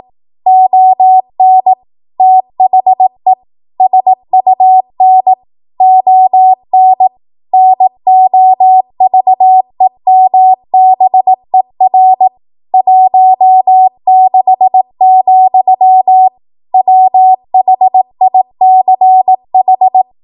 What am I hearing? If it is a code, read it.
Morse code, heard as a keyed tone, about 18 wpm, 750 Hz. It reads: ON THE SUN ON NOVEMBER 16, WHICH
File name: Archive / 18 WPM / 18WPM